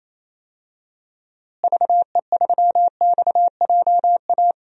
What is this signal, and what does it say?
Morse code, heard as a keyed tone, about 28 wpm, 700 Hz. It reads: VE3XJA
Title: DXCC-100 Call Signs - 6 Characters - Repeated 2x - 28wpm